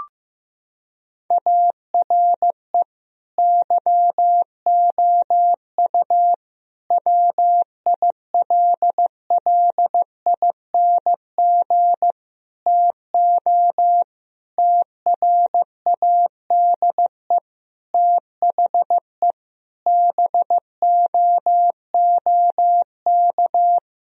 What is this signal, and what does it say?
Morse code, heard as a keyed tone, about 15 wpm, 700 Hz. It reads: ARE YOU WILLING TO TRADE THE BOOK
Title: Sentences from Top 1000 Words - Review Part 4 - 15wpm